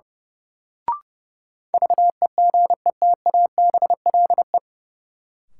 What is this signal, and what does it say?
Morse code, heard as a keyed tone, about 30 wpm, 700 Hz. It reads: VEGETABLE
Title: Sentences from Top 1300 Words - New Word Drill Alt - 30wpm